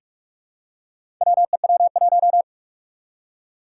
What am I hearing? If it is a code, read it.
Morse code, heard as a keyed tone, about 46 wpm, 700 Hz. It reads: WEW1
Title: Four Challenging Letter-Number Combinations 45wpm